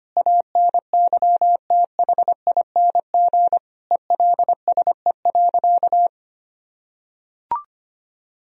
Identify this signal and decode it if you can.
Morse code, heard as a keyed tone, about 25 wpm, 700 Hz. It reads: ANYTHING ELSE.
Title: Sentences from Top 1000 Words - Review Part 1 - 25wpm